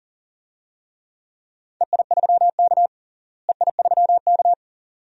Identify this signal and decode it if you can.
Morse code, heard as a keyed tone, about 40 wpm, 700 Hz. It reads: EI3K EI3K
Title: DXCC-100 Call Signs - 4 Characters - Repeated 2x - 40wpm